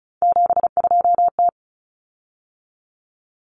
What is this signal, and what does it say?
Morse code, heard as a keyed tone, about 34 wpm, 700 Hz. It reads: 72T
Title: Common Cut Numbers - 3-Digit 35wpm